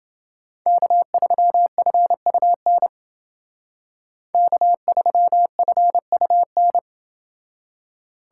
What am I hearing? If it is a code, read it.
Morse code, heard as a keyed tone, about 29 wpm, 700 Hz. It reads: K3FUN K3FUN